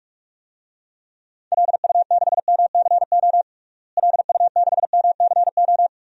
Morse code, heard as a keyed tone, about 45 wpm, 700 Hz. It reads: LU6MCO LU6MCO
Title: DXCC-100 Call Signs - 6 Characters - Repeated 2x - 45wpm